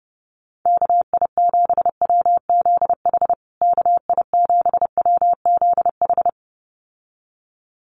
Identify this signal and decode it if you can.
Morse code, heard as a keyed tone, about 30 wpm, 700 Hz. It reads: KI7WZH KI7WZH
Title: USA Call Signs Repeated 2x 30wpm